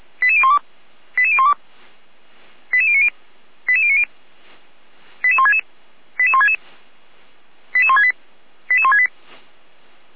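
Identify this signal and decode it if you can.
Five-tone selective call (ZVEI), heard as quick runs of five tones, tones of about 70 ms. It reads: 88012 88012 88090 88090 88166 88166 88168 88168
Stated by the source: ZVEI five-tone selective call received on SDR.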